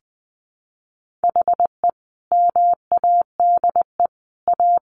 Morse code, heard as a keyed tone, about 20 wpm, 700 Hz. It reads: HE MADE A
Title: Sentences from Top 800 Words 20wpm